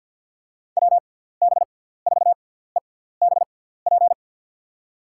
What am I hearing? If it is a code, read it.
Morse code, heard as a keyed tone, about 50 wpm, spaced as for 17 wpm, 700 Hz. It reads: WB4EBP
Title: USA Call Signs - 6x Character Spacing - 50wpm